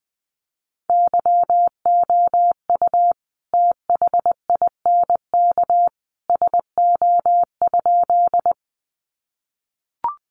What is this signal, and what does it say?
Morse code, heard as a keyed tone, about 20 wpm, 700 Hz. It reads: YOU THINK SO?
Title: Sentences from Top 200 Words 20wpm